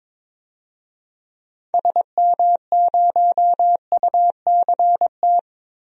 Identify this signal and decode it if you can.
Morse code, heard as a keyed tone, about 22 wpm, 700 Hz. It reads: SM0UCT
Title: DXCC-100 Call Signs - 6 Characters - Repeated 3x - 22wpm